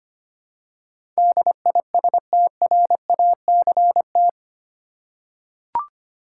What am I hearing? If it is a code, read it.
Morse code, heard as a keyed tone, about 25 wpm, 700 Hz. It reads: DISTRACT
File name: 8 Letter Words - First 3 Letters Given - 25wpm